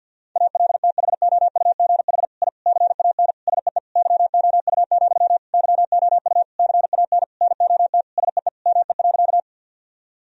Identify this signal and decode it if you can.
Morse code, heard as a keyed tone, about 50 wpm, 700 Hz. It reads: ALTHOUGH I CAN SEE YOU, YOU CAN NOT SEE ME.
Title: Sentences from Top 1000 Words - Review Part 3 - 50wpm